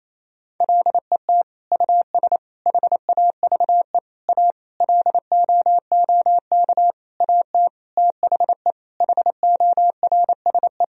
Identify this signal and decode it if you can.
Morse code, heard as a keyed tone, about 28 wpm, 700 Hz. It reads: LET US HAVE A LOOK AT THE HORSE
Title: Sentences from Top 1000 Words - Review Part 1 - 28wpm